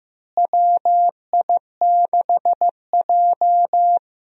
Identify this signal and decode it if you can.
Morse code, heard as a keyed tone, about 15 wpm, 700 Hz. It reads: WI6J
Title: USA Call Signs - 4 Characters - Repeated 2x - 15wpm